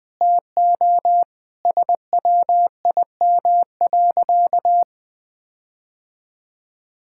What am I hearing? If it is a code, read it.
Morse code, heard as a keyed tone, about 19 wpm, 700 Hz. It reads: TO SWIM.
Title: Sentences with 5-Word Phrases - 20wpm